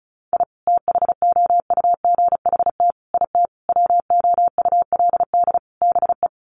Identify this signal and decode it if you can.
Morse code, heard as a keyed tone, about 34 wpm, 700 Hz. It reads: I THOUGHT IT WOULD BE
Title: Sets of 5 Words Encore 35wpm